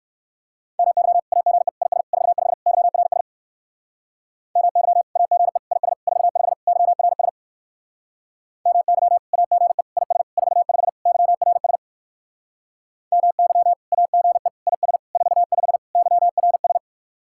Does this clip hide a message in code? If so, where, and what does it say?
Yes: Morse code from 0.8 to 16.8 s, heard as a keyed tone, about 59 wpm, 700 Hz. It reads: MY AGE IS 45 YRS MY AGE IS 45 YRS MY AGE IS 45 YRS MY AGE IS 45 YRS